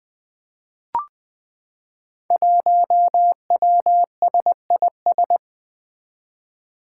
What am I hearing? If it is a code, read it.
Morse code, heard as a keyed tone, about 20 wpm, 700 Hz. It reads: 1WSIS
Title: Five Challenging Letter-Number Combinations 20wpm